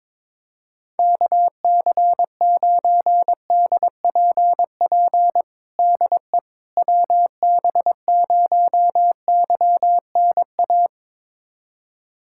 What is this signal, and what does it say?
Morse code, heard as a keyed tone, about 22 wpm, 700 Hz. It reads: KC9DPP DE WB0YNA
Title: USA Callsign DE Callsign - 6 Characters - 22wpm